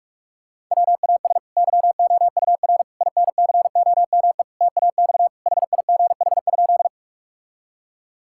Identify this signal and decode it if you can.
Morse code, heard as a keyed tone, about 46 wpm, 690 Hz. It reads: WAS YOUR INCOME TAX HIGH?
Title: Sentences from Top 1000 Words - Review Part 5 - 45wpm